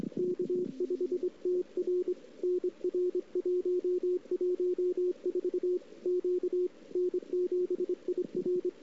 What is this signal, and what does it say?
Morse code, heard as a keyed tone, about 25 wpm, 360 Hz. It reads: K5TR NR 114 Q N7IR
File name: CW / 2008-ss-cw-k5tr-op-n5rz -00:00